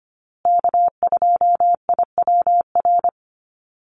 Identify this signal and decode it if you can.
Morse code, heard as a keyed tone, about 25 wpm, 700 Hz. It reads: K2IWR
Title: USA Call Signs Repeated 3x 25wpm